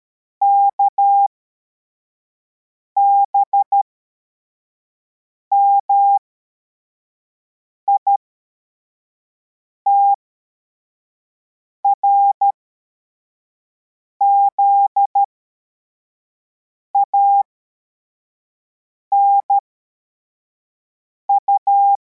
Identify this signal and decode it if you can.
Morse code, heard as a keyed tone, about 13 wpm, spaced as for 4 wpm, 800 Hz. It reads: KBMIT RZANU
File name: formation / cw / Text43